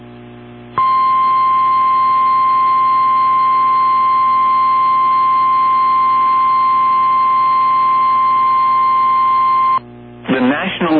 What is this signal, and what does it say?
Emergency Alert System (SAME) tones (the 1050 Hz attention tone), with speech. It is heard from 0.7 to 9.8 s.
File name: eas / KWBE / KWBE